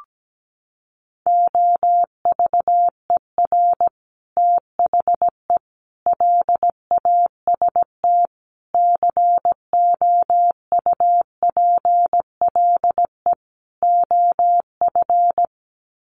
Morse code, heard as a keyed tone, about 17 wpm, 700 Hz. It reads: OVER THE LAST COUPLE OF
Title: Sets of 5 Words Encore 17wpm